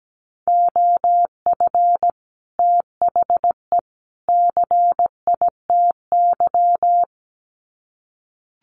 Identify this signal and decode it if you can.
Morse code, heard as a keyed tone, about 17 wpm, 700 Hz. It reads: OF THE CITY
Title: Sets of 3 Words 17wpm